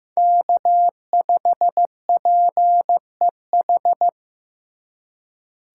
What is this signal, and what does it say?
Morse code, heard as a keyed tone, about 15 wpm, 700 Hz. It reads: K5PEH
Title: USA Call Signs - 5 Characters - 2x Character Spacing - 15wpm